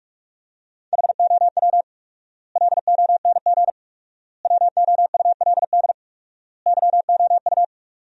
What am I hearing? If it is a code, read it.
Morse code, heard as a keyed tone, about 46 wpm, 700 Hz. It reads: HOW LONG WOULD YOU
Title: 4x Word-Spacing - Sentences from Top 500 Words - 45wpm